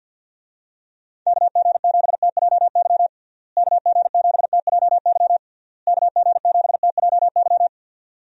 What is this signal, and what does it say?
Morse code, heard as a keyed tone, about 48 wpm, 700 Hz. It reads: KG7TJY KG7TJY KG7TJY
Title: USA Call Signs - 6 Characters - Repeated 3x - 50wpm